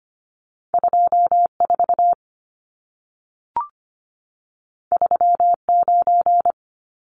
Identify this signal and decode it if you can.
Morse code, heard as a keyed tone, about 25 wpm, 700 Hz. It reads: 24 39
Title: Numbers - 2-Digit 25wpm